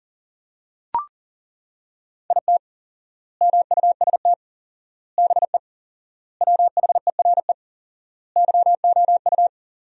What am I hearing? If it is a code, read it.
Morse code, heard as a keyed tone, about 40 wpm, 700 Hz. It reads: IT MUST BE WHERE YOU
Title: 4x Word-Spacing - Sentences from Top 200 Words - 40wpm